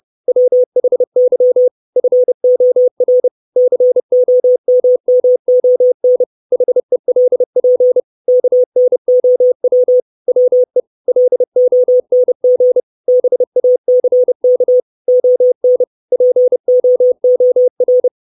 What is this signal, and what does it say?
Morse code, heard as a keyed tone, about 30 wpm, 500 Hz. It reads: WHY FOR COMMON HELP KNOW WE LONG BACK ON POOR
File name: 632 Words 30WPM